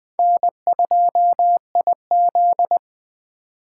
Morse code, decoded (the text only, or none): N2IZ